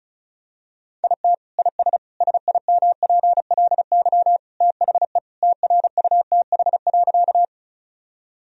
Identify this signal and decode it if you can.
Morse code, heard as a keyed tone, about 34 wpm, 700 Hz. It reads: IT IS SIMPLY THE TRUTH.
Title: Sentences from Top 1000 Words - Review Part 4 - 35wpm